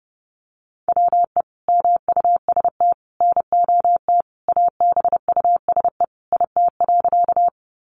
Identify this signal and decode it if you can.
Morse code, heard as a keyed tone, about 30 wpm, 700 Hz. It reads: WE MUST NOT ABUSE IT.
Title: Sentences from Top 1300 Words 30wpm